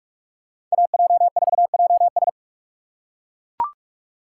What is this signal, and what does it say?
Morse code, heard as a keyed tone, about 41 wpm, 700 Hz. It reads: AJ4JS
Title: USA Call Signs - 5 Characters - 2x Character Spacing - 45wpm